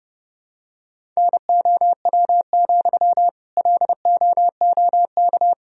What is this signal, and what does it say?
Morse code, heard as a keyed tone, about 30 wpm, 700 Hz. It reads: NOW, LOOK
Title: Sentences from Top 1000 Words - Review Part 4 - 30wpm